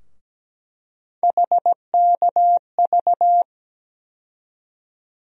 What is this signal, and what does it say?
Morse code, heard as a keyed tone, about 17 wpm, 700 Hz. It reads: HKV
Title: USA Call Sign Postfixes - 3 Letters 17wpm